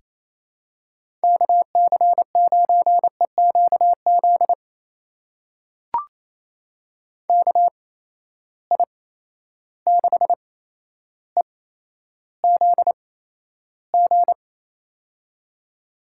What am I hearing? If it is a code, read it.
Morse code, heard as a keyed tone, about 27 wpm, spaced as for 12 wpm, 700 Hz. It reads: KC9EQZ KI6EZG
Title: USA Call Signs - 6 Characters - 8x Character Spacing - 28wpm